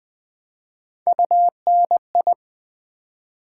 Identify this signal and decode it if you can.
Morse code, heard as a keyed tone, about 20 wpm, 700 Hz. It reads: UNI
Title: Build A Word - 3 Letters At A Time - 6-Letter Words - 20wpm